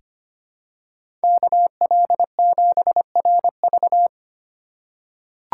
Morse code, heard as a keyed tone, about 25 wpm, 700 Hz. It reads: KL7RV
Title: USA Call Signs - 5 Characters - 2x Character Spacing - 25wpm